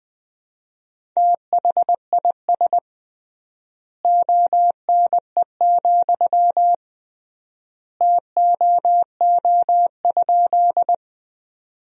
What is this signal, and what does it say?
Morse code, heard as a keyed tone, about 20 wpm, 700 Hz. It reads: THIS ONE, TOO?
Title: 3x Word-Spacing - Sentences from Top 200 Words - 20wpm